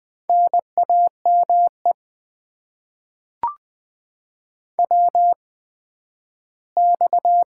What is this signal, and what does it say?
Morse code, heard as a keyed tone, about 20 wpm, 700 Hz. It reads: N A M E W X
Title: Top 90 QSO Elements - 8x Character Spacing - 20wpm